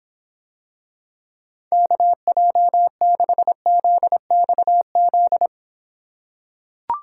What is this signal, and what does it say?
Morse code, heard as a keyed tone, about 24 wpm, 700 Hz. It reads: KJ6ZXZ